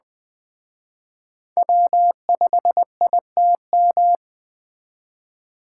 Morse code, heard as a keyed tone, about 20 wpm, 700 Hz. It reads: W5ITM
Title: Speed Racing - USA 1x3 Call Signs 17wpm